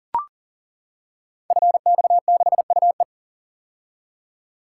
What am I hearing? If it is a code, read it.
Morse code, heard as a keyed tone, about 39 wpm, 700 Hz. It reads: FX6UE
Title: Code Groups - 5 Letter:Numbers - 40wpm